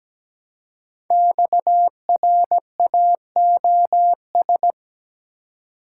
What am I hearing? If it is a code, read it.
Morse code, heard as a keyed tone, about 17 wpm, 700 Hz. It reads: XRAOS
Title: Code Groups - 5 Letters - 17wpm